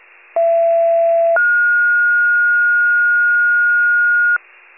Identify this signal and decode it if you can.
Two-tone page, heard as a one-second tone, then a three-second tone, about 670 then 1430 Hz. It is heard from 0.4 to 4.4 s.
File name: pager/21 RESCUE